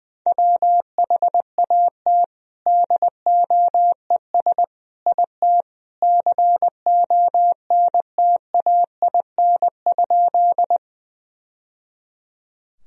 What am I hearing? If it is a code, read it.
Morse code, heard as a keyed tone, about 21 wpm, 700 Hz. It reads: WHAT DOES IT CONTAIN?
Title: Sentences from Top 1000 Words - Review Part 2 - 20wpm